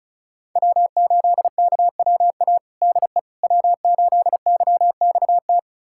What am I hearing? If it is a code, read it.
Morse code, heard as a keyed tone, about 36 wpm, 700 Hz. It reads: W8KWA DE W8YXT
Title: USA Callsign DE Callsign - 5 Characters - 35wpm